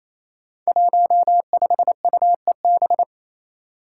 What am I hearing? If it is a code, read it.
Morse code, heard as a keyed tone, about 28 wpm, 700 Hz. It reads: 15UEB
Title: Five Challenging Letter-Number Combinations 28wpm